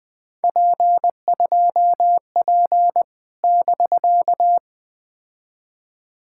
Morse code, heard as a keyed tone, about 20 wpm, 700 Hz. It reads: P2P <BK>
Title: POTA QSO Practice - 20wpm